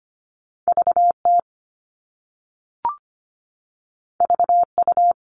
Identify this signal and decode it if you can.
Morse code, heard as a keyed tone, about 25 wpm, 700 Hz. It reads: VT VU
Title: All Cut Numbers - 2-Digit 25wpm